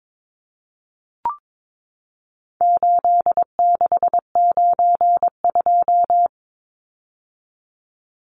Morse code, heard as a keyed tone, about 22 wpm, 700 Hz. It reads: 8692
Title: Numbers - 4-Digit 22wpm